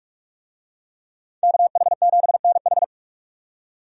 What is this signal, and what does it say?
Morse code, heard as a keyed tone, about 46 wpm, 700 Hz. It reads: KH7NH
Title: USA Call Signs - 5 Characters - 45wpm